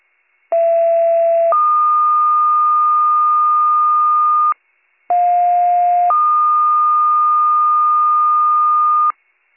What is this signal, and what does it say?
Two-tone page, heard as a one-second tone, then a three-second tone, about 670 then 1220 Hz, about 710 then 1220 Hz. It is heard from 0.5 to 9.1 s.